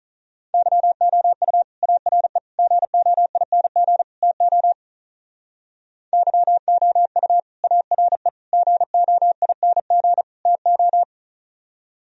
Morse code, heard as a keyed tone, about 43 wpm, 700 Hz. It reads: YOU ARE GOING TO YOU ARE GOING TO
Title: Speed-Racing - Sets of 4 Words 35wpm